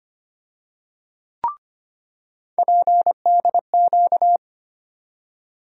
Morse code, heard as a keyed tone, about 25 wpm, 700 Hz. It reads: PDQ
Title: USA Call Sign Postfixes - 3 Letters 25wpm